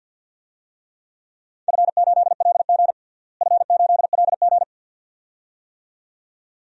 Morse code, heard as a keyed tone, about 52 wpm, 700 Hz. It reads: F8LG F8LG
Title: DXCC-100 Call Signs - 4 Characters - Repeated 2x - 50wpm